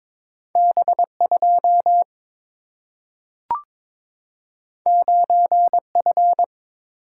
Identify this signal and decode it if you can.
Morse code, heard as a keyed tone, about 22 wpm, 700 Hz. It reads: B2 9F
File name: practice / 22wpm / Two Letter-Number Combinations - 22wpm